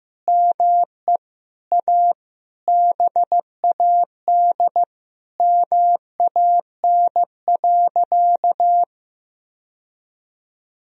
Morse code, heard as a keyed tone, about 15 wpm, 700 Hz. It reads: ME A BAD MAN.
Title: Sentences from Top 1000 Words - Review Part 4 - 15wpm